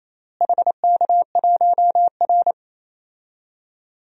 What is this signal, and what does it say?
Morse code, heard as a keyed tone, about 28 wpm, 700 Hz. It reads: HK1R
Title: DXCC-100 Call Signs - 4 Characters - Repeated 3x - 28wpm